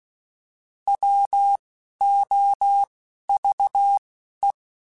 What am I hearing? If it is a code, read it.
Morse code, heard as a keyed tone, about 16 wpm, 800 Hz. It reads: W O V E
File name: Alphabetical / Alphabetical mp3s / Ws @10